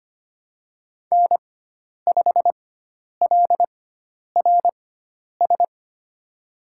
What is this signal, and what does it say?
Morse code, heard as a keyed tone, about 25 wpm, spaced as for 10 wpm, 700 Hz. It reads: N5LRS
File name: Lesson 16g - Letter c - Call Signs 25wpm